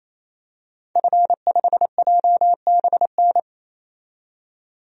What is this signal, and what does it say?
Morse code, heard as a keyed tone, about 28 wpm, 700 Hz. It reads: F5JBN
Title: DXCC-100 Call Signs - 5 Characters - 28wpm